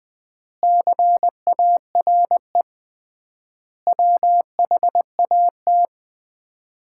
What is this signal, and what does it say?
Morse code, heard as a keyed tone, about 20 wpm, 700 Hz. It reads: CARE WHAT
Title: 3x Word-Spacing - Sets of 5 Words - 20wpm